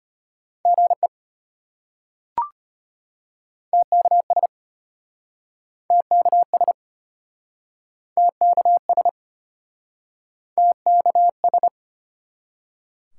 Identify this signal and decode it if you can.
Morse code, heard as a keyed tone, about 33 wpm, 700 Hz. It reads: GE TKS TKS TKS TKS